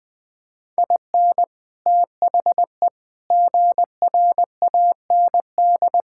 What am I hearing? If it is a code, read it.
Morse code, heard as a keyed tone, about 20 wpm, 700 Hz. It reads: IN THE GRAND
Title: Sentences with 6-Word Phrases - Spoken Beginning - 20wpm